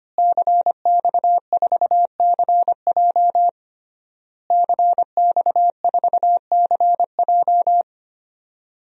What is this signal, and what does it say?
Morse code, heard as a keyed tone, about 25 wpm, 700 Hz. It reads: CX4CJ CX4CJ
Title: DXCC-100 Call Signs - 5 Characters - Repeated 3x - 25wpm